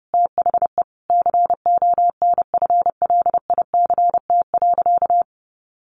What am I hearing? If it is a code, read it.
Morse code, heard as a keyed tone, about 30 wpm, 700 Hz. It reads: THE CONFLICT.
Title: Sentences from Top 1300 Words 30wpm